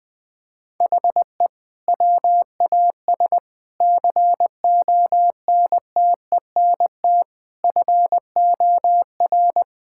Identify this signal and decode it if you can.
Morse code, heard as a keyed tone, about 21 wpm, 700 Hz. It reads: HE WAS CONTENT FOR
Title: Sentences from Top 1000 Words - Review Part 6 - 20wpm